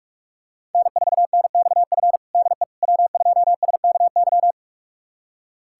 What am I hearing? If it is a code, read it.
Morse code, heard as a keyed tone, about 46 wpm, 700 Hz. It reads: N4NXF DE W2SKY
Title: USA Callsign DE Callsign - 5 Characters - 45wpm